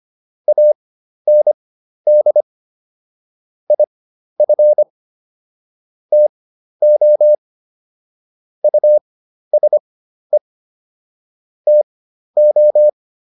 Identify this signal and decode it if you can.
Morse code, heard as a keyed tone, about 25 wpm, spaced as for 12 wpm, 600 Hz. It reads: AND IF TO USE TO